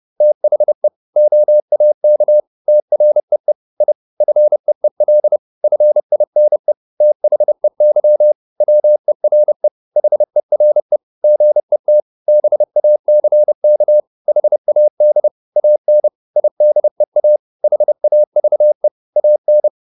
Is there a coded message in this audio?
Yes: Morse code from 0.2 to 19.7 s, heard as a keyed tone, about 30 wpm, 600 Hz. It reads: THE OAK TREE I FEEL FINE THEY WERE HERE GET BACK HAD AN IDEA HAVE AN